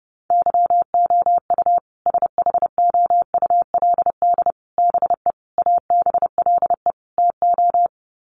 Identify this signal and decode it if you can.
Morse code, heard as a keyed tone, about 30 wpm, 700 Hz. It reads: YOU SHOULD BE ABLE TO
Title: Sets of 5 Words Encore 30wpm